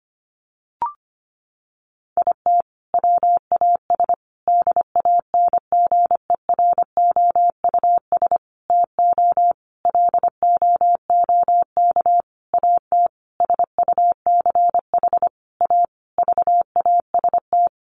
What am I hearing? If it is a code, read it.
Morse code, heard as a keyed tone, about 25 wpm, 700 Hz. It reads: IT WAS DANGEROUS TO LOOK AT SUCH A VAST
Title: Sentences from Top 1300 Words 25wpm